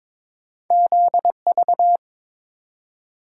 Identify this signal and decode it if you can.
Morse code, heard as a keyed tone, about 22 wpm, 700 Hz. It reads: ZV
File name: practice / 22wpm / Two Letter-Number Combinations - 22wpm